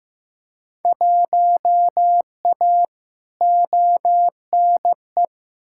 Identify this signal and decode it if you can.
Morse code, heard as a keyed tone, about 15 wpm, 700 Hz. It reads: 1A ONE
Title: Field Day Exchanges Encore - 15wpm